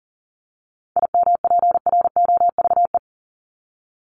Morse code, heard as a keyed tone, about 40 wpm, 700 Hz. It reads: IMPROVE
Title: Top 501-750 Words 40wpm